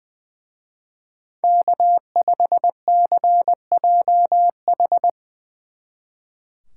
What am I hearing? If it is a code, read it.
Morse code, heard as a keyed tone, about 22 wpm, 700 Hz. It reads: K5CJH
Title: Speed Racing - USA 1x3 Call Signs 20wpm